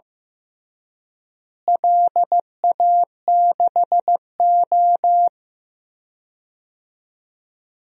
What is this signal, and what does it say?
Morse code, heard as a keyed tone, about 15 wpm, 700 Hz. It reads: LA6O
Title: DXCC-100 Call Signs - 4 Characters - Repeated 3x - 15wpm